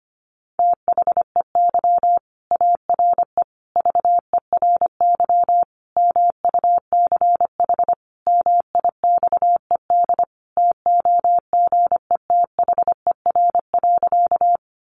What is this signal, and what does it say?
Morse code, heard as a keyed tone, about 25 wpm, 700 Hz. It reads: THEY ARE VERY MUCH MIXED TOGETHER.
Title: Sentences from Top 1500 Words 25wpm